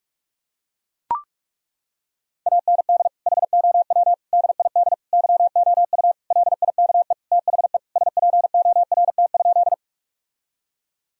Morse code, heard as a keyed tone, about 41 wpm, 700 Hz. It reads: AND HOW DID YOU LIKE THE SPORT?
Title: Sentences from Top 1000 Words - Review Part 4 - 45wpm